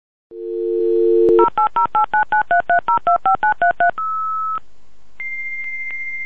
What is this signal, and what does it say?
DTMF, heard as telephone touch tones, tones of about 120 ms. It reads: 08089933025933